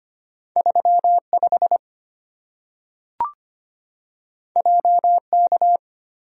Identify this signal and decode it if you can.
Morse code, heard as a keyed tone, about 25 wpm, 700 Hz. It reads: 35 JK
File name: Two Letter-Number Combinations - 25wpm